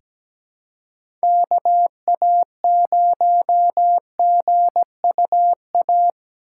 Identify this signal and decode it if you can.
Morse code, heard as a keyed tone, about 17 wpm, 700 Hz. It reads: KA0GUA
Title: USA Call Signs - 6 Characters - 6x Character Spacing - 17wpm